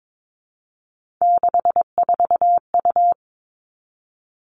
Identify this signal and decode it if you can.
Morse code, heard as a keyed tone, about 22 wpm, 700 Hz. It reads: 64U
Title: Three Challenging Letter-Number Combinations 22wpm